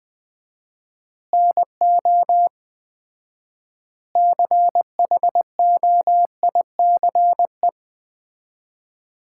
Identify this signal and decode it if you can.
Morse code, heard as a keyed tone, about 20 wpm, 700 Hz. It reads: NO CHOICE
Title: 4x Word-Spacing - Sets of 5 Words - 20wpm